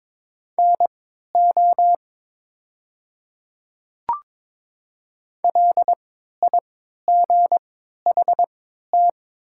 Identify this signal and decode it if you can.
Morse code, heard as a keyed tone, about 22 wpm, spaced as for 13 wpm, 690 Hz. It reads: NO LIGHT
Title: 3x Character Spacing - Top 200 Words - 22wpm